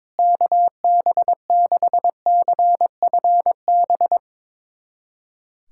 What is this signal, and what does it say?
Morse code, heard as a keyed tone, about 22 wpm, 700 Hz. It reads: KB6CFB
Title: USA Callsign DE Callsign 22wpm